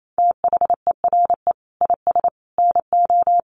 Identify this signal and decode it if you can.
Morse code, heard as a keyed tone, about 28 wpm, 700 Hz. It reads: THERE IS NO
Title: Sets of 5 Words Encore 28wpm